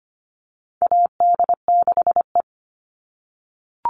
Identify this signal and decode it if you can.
Morse code, heard as a keyed tone, about 25 wpm, 700 Hz. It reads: AD6E
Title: Call Signs - 25wpm